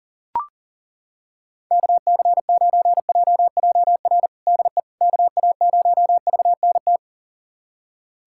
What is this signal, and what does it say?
Morse code, heard as a keyed tone, about 40 wpm, 700 Hz. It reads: KC9JJR DE KA0VNT
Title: USA Callsign DE Callsign - 6 Characters - 40wpm